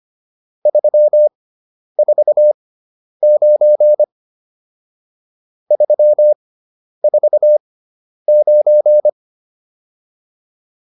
Morse code, heard as a keyed tone, about 25 wpm, spaced as for 10 wpm, 600 Hz. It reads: 349 349
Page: intermediate-practice-files